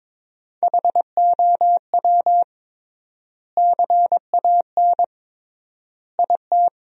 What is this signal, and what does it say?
Morse code, heard as a keyed tone, about 22 wpm, 700 Hz. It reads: HOW CAN IT
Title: 3x Word-Spacing - Sentences from Top 100 Words - 22wpm